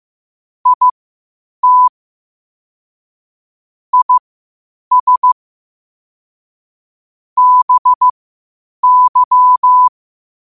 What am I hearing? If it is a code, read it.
Morse code, heard as a keyed tone, about 15 wpm, spaced as for 8 wpm, 1000 Hz. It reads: IT IS BY
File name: qotdfiles / 8wpm